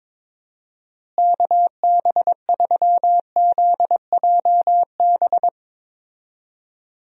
Morse code, heard as a keyed tone, about 22 wpm, 700 Hz. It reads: KB3ZJB
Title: USA Call Signs - 6 Characters - Repeated 3x - 22wpm